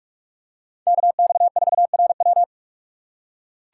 Morse code, heard as a keyed tone, about 42 wpm, 700 Hz. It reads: KX4RW